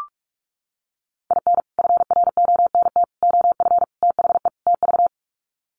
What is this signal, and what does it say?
Morse code, heard as a keyed tone, about 44 wpm, 700 Hz. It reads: IN FRONT OF THE TV
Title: Sets of 5 Words Encore2 45wpm